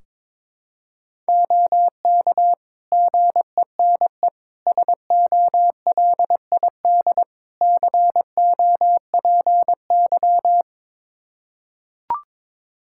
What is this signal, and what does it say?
Morse code, heard as a keyed tone, about 22 wpm, 700 Hz. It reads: OK GENE SOLID COPY
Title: Practice QSO - Exchange 3 - More Info - 22wpm